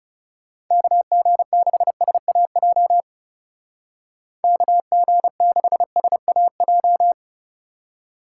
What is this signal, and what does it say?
Morse code, heard as a keyed tone, about 37 wpm, 700 Hz. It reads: KG6SAJ KG6SAJ
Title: Speed Racing - USA 2x3 Call Signs 30wpm